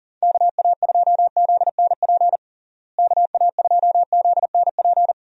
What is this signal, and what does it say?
Morse code, heard as a keyed tone, about 40 wpm, 700 Hz. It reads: KA2ZNP KA2ZNP
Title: USA Call Signs - 6 Characters - Repeated 2x - 40wpm